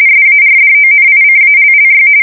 RTTY, about 45.45 baud. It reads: SARDONICALLY
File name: fsk170 / USS-New-Jersey